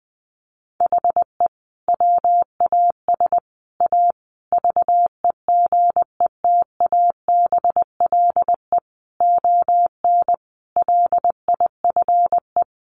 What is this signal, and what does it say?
Morse code, heard as a keyed tone, about 20 wpm, 700 Hz. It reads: HE WAS A VEGETABLE ON LIFE
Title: Sentences from Top 1300 Words 20wpm